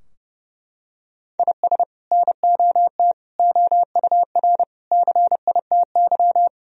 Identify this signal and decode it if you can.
Morse code, heard as a keyed tone, about 32 wpm, 700 Hz. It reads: IS NOT OUR CITY
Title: Sentences from Top 1000 Words - Review Part 3 - 30wpm